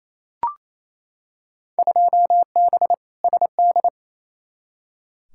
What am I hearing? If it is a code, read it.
Morse code, heard as a keyed tone, about 28 wpm, 700 Hz. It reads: 2B SD
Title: Field Day Exchanges Encore - 28wpm